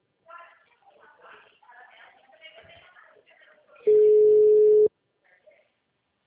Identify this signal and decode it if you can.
Ringback tone (Europe, 425 Hz), with speech.
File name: out- -2020- - - .33079